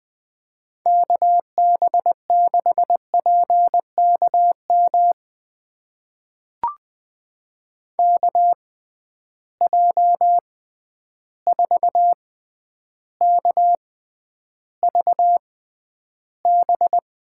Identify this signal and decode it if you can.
Morse code, heard as a keyed tone, about 20 wpm, spaced as for 10 wpm, 700 Hz. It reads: KB6PKM KJ4KVB
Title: USA Call Signs - 6 Characters - 6x Character Spacing - 20wpm